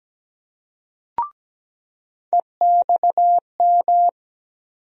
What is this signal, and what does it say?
Morse code, heard as a keyed tone, about 17 wpm, 700 Hz. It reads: EXM
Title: USA Call Sign Postfixes - 3 Letters 17wpm